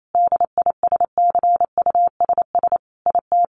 Morse code, heard as a keyed tone, about 28 wpm, 700 Hz. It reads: DISCUSS IT
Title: Sentences from Top 1100 Words 28wpm